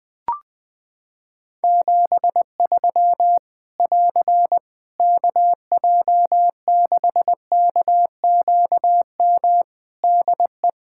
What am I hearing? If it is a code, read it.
Morse code, heard as a keyed tone, about 21 wpm, 700 Hz. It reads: 73 <AR> KJ6KQM DE
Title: Practice QSO - Exchange 4 - Ending - 20wpm